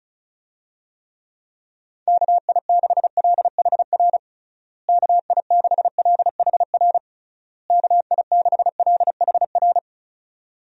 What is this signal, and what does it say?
Morse code, heard as a keyed tone, about 34 wpm, 700 Hz. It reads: KI6LHR KI6LHR KI6LHR
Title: USA Call Signs - 6 Characters - Repeated 3x - 35wpm